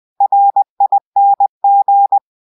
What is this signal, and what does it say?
Morse code, heard as a keyed tone, about 20 wpm, 800 Hz. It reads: RING
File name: go-download